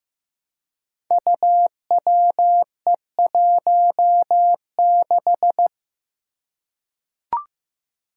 Morse code, heard as a keyed tone, about 15 wpm, 700 Hz. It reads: UWE16
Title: Five Challenging Letter-Number Combinations 15wpm